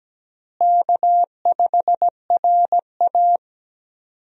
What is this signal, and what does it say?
Morse code, heard as a keyed tone, about 17 wpm, 700 Hz. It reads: K5RA
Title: USA Call Signs - 4 Characters - Repeated 2x - 17wpm